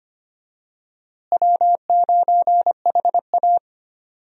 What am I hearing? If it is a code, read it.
Morse code, heard as a keyed tone, about 25 wpm, 700 Hz. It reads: W9HA
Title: USA Call Signs - 4 Characters - Repeated 3x - 25wpm